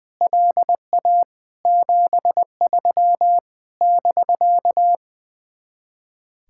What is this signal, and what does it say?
Morse code, heard as a keyed tone, about 21 wpm, 700 Hz. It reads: LA 73 <BK>
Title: Speed-Racing Common QSO Phrases Encore - 20wpm